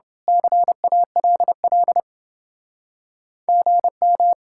Morse code, heard as a keyed tone, about 30 wpm, 700 Hz. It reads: CALL GM